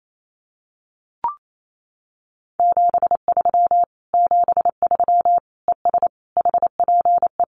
Morse code, heard as a keyed tone, about 28 wpm, 700 Hz. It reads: 73 73 ES HPE
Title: Common QSO Phrases Encore 28wpm